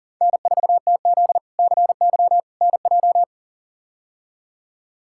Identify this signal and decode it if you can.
Morse code, heard as a keyed tone, about 40 wpm, 700 Hz. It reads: N4TZ CY NJ
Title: K1USN SST QSO Practice - 40wpm